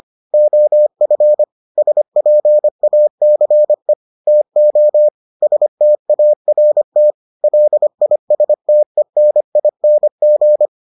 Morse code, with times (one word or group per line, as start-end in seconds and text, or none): 0.3-1.4 OF
1.8-3.9 SPACE
4.3-5.1 TO
5.4-7.1 START
7.4-10.7 LISTENING